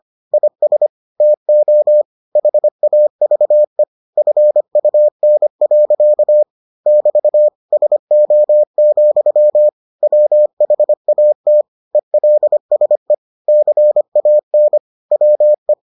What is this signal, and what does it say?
Morse code, heard as a keyed tone, about 25 wpm, 600 Hz. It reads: IS TO HAVE FUN. <BT> SO, WHAT ELSE CAN WE